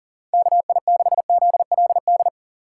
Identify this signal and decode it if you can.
Morse code, heard as a keyed tone, about 40 wpm, 700 Hz. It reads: KI6ZLD
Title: USA Call Signs - 4x Character Spacing - 40wpm